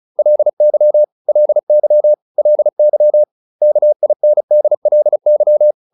Morse code, heard as a keyed tone, about 34 wpm, 600 Hz. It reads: LY LY LY KINDLY